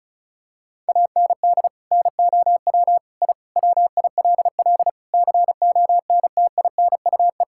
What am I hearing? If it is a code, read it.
Morse code, heard as a keyed tone, about 34 wpm, 700 Hz. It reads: AND NOW I WILL CONTINUE
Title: Sentences from Top 1000 Words - Review Part 6 - 35wpm